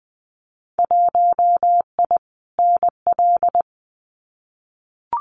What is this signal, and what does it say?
Morse code, heard as a keyed tone, about 21 wpm, 700 Hz. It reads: 1I NL
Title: Winter Field Day Exchanges Encore 20wpm